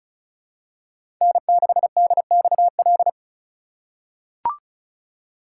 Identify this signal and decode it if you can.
Morse code, heard as a keyed tone, about 36 wpm, 700 Hz. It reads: N6DXL
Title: USA Call Signs - 5 Characters - 4x Character Spacing - 35wpm